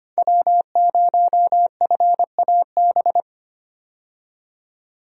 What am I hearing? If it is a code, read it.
Morse code, heard as a keyed tone, about 25 wpm, 700 Hz. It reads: W0FAB
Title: USA Call Signs - 5 Characters - 25wpm